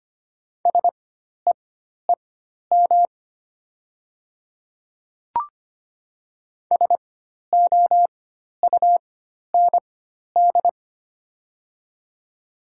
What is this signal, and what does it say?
Morse code, heard as a keyed tone, about 25 wpm, spaced as for 12 wpm, 700 Hz. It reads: SEEM SOUND
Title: 4x Character Spacing - Top 300 Words - 25wpm